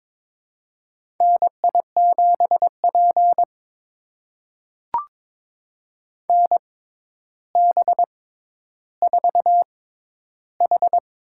Morse code, heard as a keyed tone, about 22 wpm, spaced as for 11 wpm, 700 Hz. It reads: NI7P NB4H
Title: USA Call Signs - 4 Characters - 6x Character Spacing - 22wpm